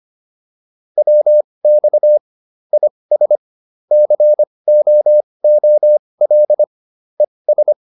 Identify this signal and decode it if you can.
Morse code, heard as a keyed tone, about 25 wpm, spaced as for 20 wpm, 600 Hz. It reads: WX IS COOL ES